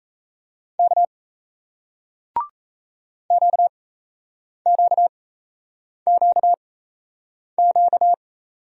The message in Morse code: K QQQQ

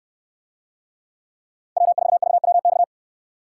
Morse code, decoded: W4VWX

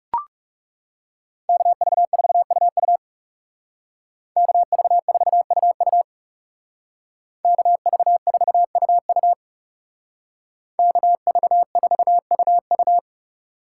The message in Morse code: KV4UU KV4UU KV4UU KV4UU

